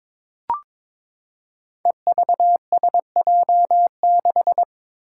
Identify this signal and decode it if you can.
Morse code, heard as a keyed tone, about 22 wpm, 700 Hz. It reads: EVSJ6
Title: Five Challenging Letter-Number Combinations 22wpm